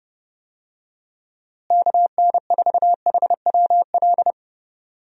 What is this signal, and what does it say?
Morse code, heard as a keyed tone, about 30 wpm, 700 Hz. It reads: KN4HWL
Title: USA Call Signs - 6 Characters - Repeated 3x - 30wpm